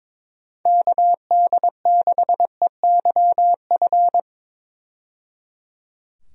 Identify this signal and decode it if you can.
Morse code, heard as a keyed tone, about 22 wpm, 700 Hz. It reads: KD6EYF
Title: USA Call Signs - 6 Characters - 22wpm